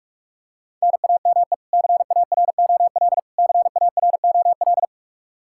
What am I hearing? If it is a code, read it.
Morse code, heard as a keyed tone, about 46 wpm, 700 Hz. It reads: NAME CAROL CAROL
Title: Practice QSO - Exchange 1 - Beginning - 45wpm